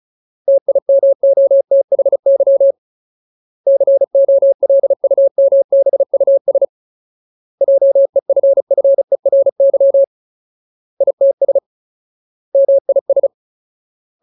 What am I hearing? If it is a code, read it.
Morse code, heard as a keyed tone, about 35 wpm, 550 Hz. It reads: TIMOTHY COLUMBUS JEFFERY ITS MIS
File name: Ultimate Word List / 35WPM Ultimate